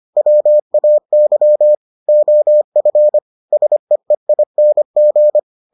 Morse code, heard as a keyed tone, about 25 wpm, 600 Hz. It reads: WAY OF SEEING